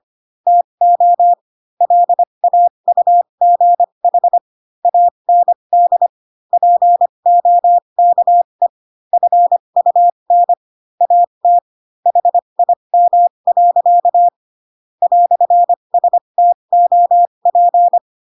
Morse code, heard as a keyed tone, about 25 wpm, spaced as for 22 wpm, 700 Hz. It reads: TO LAUGH AND POKE FUN AT HIM. "STOP
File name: Pinocchio / Pinocchio ch3 18 25 ik1hge